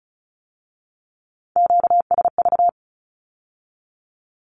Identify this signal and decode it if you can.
Morse code, heard as a keyed tone, about 36 wpm, 700 Hz. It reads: QSV
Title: Q-Codes with Spoken Meaning - 35wpm